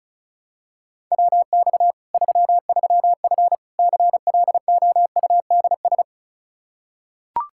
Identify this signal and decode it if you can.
Morse code, heard as a keyed tone, about 34 wpm, 700 Hz. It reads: WX 33F CLOUDS
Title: Practice QSO - Exchange 2 - Info - 35wpm